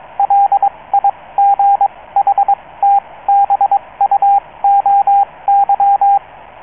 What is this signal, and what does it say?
Morse code, heard as a keyed tone, about 22 wpm, spaced as for 18 wpm, 800 Hz, with radio noise. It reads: LIGHTBUOY